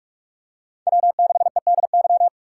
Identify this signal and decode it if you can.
Morse code, heard as a keyed tone, about 44 wpm, 700 Hz. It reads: W6EDY
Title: K1USN SST QSO Practice - 45wpm